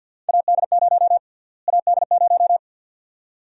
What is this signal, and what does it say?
Morse code, heard as a keyed tone, about 45 wpm, 700 Hz. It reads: AD0 AD0
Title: USA Call Sign Prefixes - Repeated 2x - 50wpm